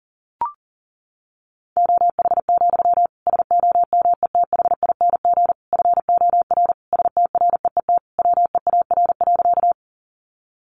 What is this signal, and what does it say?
Morse code, heard as a keyed tone, about 40 wpm, 700 Hz. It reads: OH, SOMETHING FOR STREET WEAR.
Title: Sentences from Top 1500 Words 40wpm